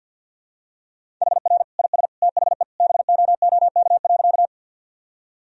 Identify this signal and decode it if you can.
Morse code, heard as a keyed tone, about 51 wpm, 700 Hz. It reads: HR IS THE BOOK.
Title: Sentences from Top 300 Words and Abbreviations 50wpm